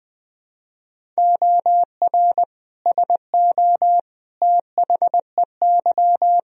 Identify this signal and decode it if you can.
Morse code, heard as a keyed tone, about 21 wpm, 700 Hz. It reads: OR SO THEY
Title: Sentences from Top 100 Words and Abbreviations 20wpm